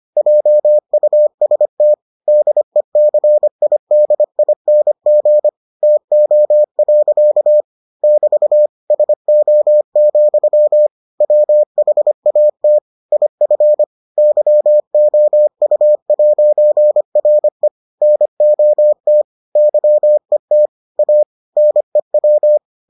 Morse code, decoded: JUST DECIDING TO. <BT> SO, WHAT IF YOU'RE NOT YET A NEW

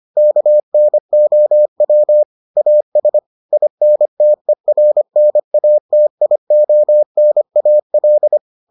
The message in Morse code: KNOW AS INTERNATIONAL

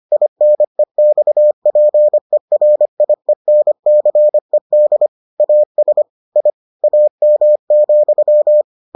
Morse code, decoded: INEXPERIENCED AS I AM,

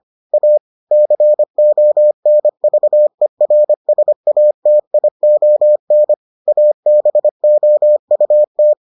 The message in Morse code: A CONVERSATION ABOUT

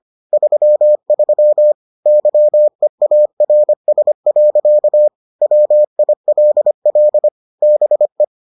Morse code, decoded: 33 YEARS. WILL BE

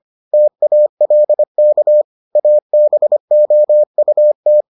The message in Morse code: TALK ABOUT

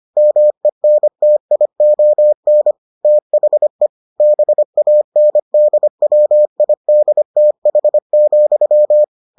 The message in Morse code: MENTION THE BANDWIDTH,

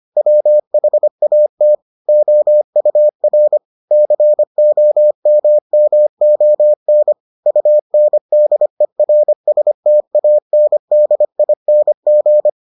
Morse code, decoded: WHAT OUR COMMON UNDERSTANDING

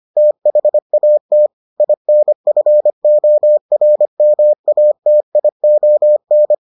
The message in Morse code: THAT INFORMATION

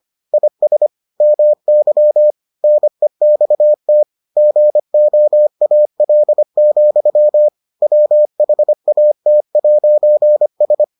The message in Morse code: IS MY NEXT GOAL, WHAT'S

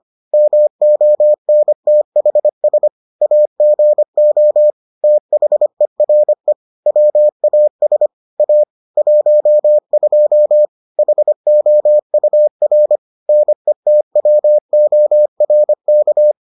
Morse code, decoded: MONTHS AGO THERE WAS A 12 HOUR NETWORK

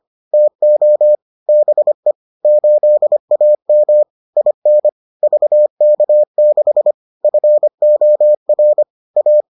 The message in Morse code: TO BE 8AM IN VK6 FOR A

